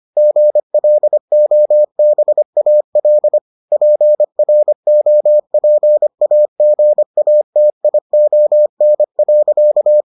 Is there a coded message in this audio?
Yes: Morse code from 0.2 to 10.0 s, heard as a keyed tone, about 25 wpm, 600 Hz. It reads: GLOBAL PROPAGATION.